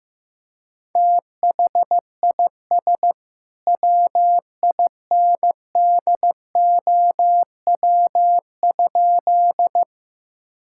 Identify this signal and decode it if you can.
Morse code, heard as a keyed tone, about 15 wpm, 700 Hz. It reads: THIS WINDOW?
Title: Sentences from Top 1600 Words 15wpm